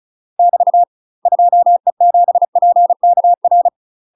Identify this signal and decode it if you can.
Morse code, heard as a keyed tone, about 36 wpm, 700 Hz. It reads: <BT> 2E7PKR
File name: UK MASTER QSOs TRAINING/35WPM